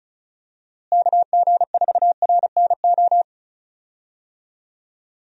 Morse code, heard as a keyed tone, about 36 wpm, 700 Hz. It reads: KG4RNO
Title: USA Call Signs - 6 Characters - Repeated 3x - 35wpm